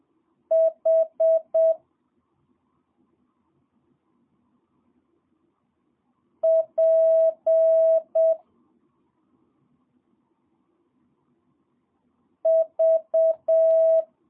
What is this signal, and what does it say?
Morse code, heard as a keyed tone, about 7 wpm, 650 Hz. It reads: H P V
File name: jh8cbh / 2022cw